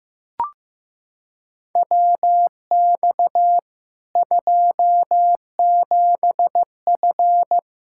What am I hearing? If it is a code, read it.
Morse code, heard as a keyed tone, about 15 wpm, 700 Hz. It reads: WX 27F
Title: K1USN SST QSO Practice 15wpm